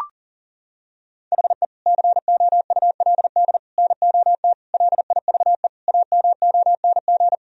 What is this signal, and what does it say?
Morse code, heard as a keyed tone, about 37 wpm, 700 Hz. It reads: HE COULD NOT LIVE AMONG
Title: Sentences from Top 1000 Words - Review Part 3 - 40wpm